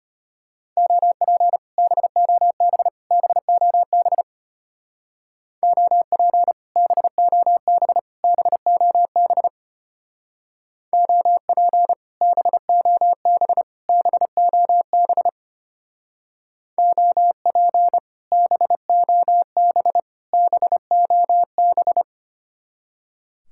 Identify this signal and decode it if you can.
Morse code, heard as a keyed tone, about 32 wpm, 700 Hz. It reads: OP BOB BOB OP BOB BOB OP BOB BOB OP BOB BOB